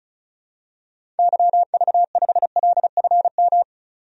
Morse code, heard as a keyed tone, about 36 wpm, 700 Hz. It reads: YV5LFM
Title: DXCC-100 Call Signs - 6 Characters - 35wpm